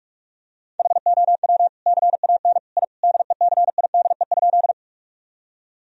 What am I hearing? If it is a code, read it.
Morse code, heard as a keyed tone, about 46 wpm, 700 Hz. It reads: HOW CAN I DECIDE?
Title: Sentences from Top 1000 Words - Review Part 3 - 45wpm